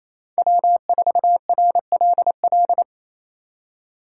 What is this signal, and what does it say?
Morse code, heard as a keyed tone, about 36 wpm, 700 Hz. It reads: W4RLL